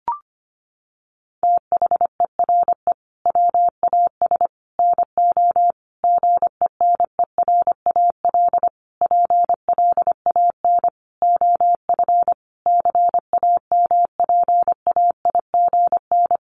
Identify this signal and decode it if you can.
Morse code, heard as a keyed tone, about 25 wpm, 700 Hz. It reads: THERE WAS NO GENERAL PLAN OF CAMPAIGN
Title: Sentences from Top 1200 Words 25wpm